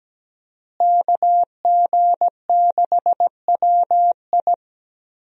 Morse code, heard as a keyed tone, about 17 wpm, 700 Hz. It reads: KG6WI
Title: USA Call Signs - 5 Characters - Repeated 3x - 17wpm